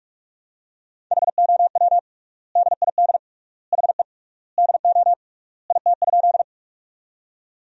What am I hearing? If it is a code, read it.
Morse code, heard as a keyed tone, about 46 wpm, 700 Hz. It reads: HOW DID HE DO IT?
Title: 3x Word-Spacing - Sentences from Top 100 Words - 45wpm